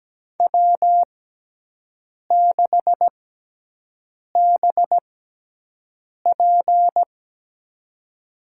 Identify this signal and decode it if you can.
Morse code, heard as a keyed tone, about 17 wpm, spaced as for 6 wpm, 700 Hz. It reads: W6BP
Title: USA Call Signs - 4 Characters - 6x Character Spacing - 17wpm